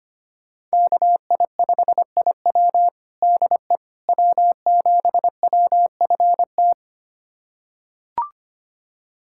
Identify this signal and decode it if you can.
Morse code, heard as a keyed tone, about 25 wpm, 700 Hz. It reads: KI5IW DE W7WFT
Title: USA Callsign DE Callsign 25wpm